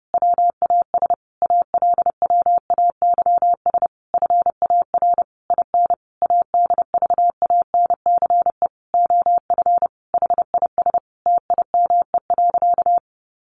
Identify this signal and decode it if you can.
Morse code, heard as a keyed tone, about 30 wpm, 700 Hz. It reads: WAS ALWAYS FAR IN ADVANCE OF HIS TIME.
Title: Sentences from Top 1400 Words 30wpm